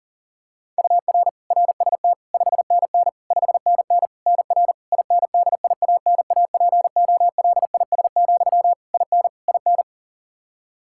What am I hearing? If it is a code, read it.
Morse code, heard as a keyed tone, about 40 wpm, 700 Hz. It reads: UR RST 5NN 5NN NR INDIANAPOLIS, IN IN
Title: K1USN SST QSO Practice 40wpm